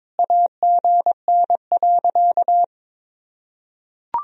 Morse code, heard as a keyed tone, about 22 wpm, 700 Hz. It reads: AGN.
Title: Sentences from Top 500 Words and Abbreviations 22wpm